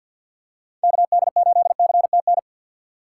Morse code, heard as a keyed tone, about 50 wpm, 700 Hz. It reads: KD8CTN
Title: USA Call Signs - 6 Characters - 6x Character Spacing - 50wpm